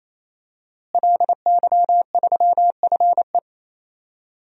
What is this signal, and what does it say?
Morse code, heard as a keyed tone, about 28 wpm, 700 Hz. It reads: LY3FE
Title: DXCC-100 Call Signs - 5 Characters - Repeated 3x - 28wpm